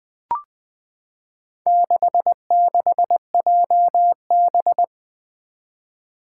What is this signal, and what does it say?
Morse code, heard as a keyed tone, about 20 wpm, 700 Hz. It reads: 66JB
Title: Four Challenging Letter-Number Combinations 20wpm